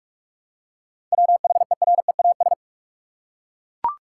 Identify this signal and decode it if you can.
Morse code, heard as a keyed tone, about 42 wpm, 700 Hz. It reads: WHEREAS
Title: Seven Letter Words - 45wpm